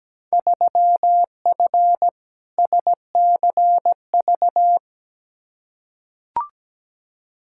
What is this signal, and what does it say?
Morse code, heard as a keyed tone, about 17 wpm, 700 Hz. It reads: 3F SCV
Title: Field Day Exchanges - 17wpm